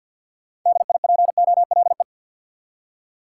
Morse code, heard as a keyed tone, about 49 wpm, 700 Hz. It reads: DIPOLE
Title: Top 180 QSO Elements 50wpm